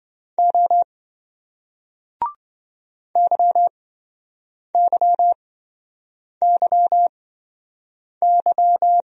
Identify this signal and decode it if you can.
Morse code, heard as a keyed tone, about 27 wpm, spaced as for 7 wpm, 700 Hz. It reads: O YYYY